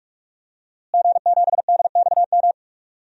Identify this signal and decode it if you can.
Morse code, heard as a keyed tone, about 46 wpm, 700 Hz. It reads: G7DXM
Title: DXCC-100 Call Signs - 5 Characters - Repeated 2x - 45wpm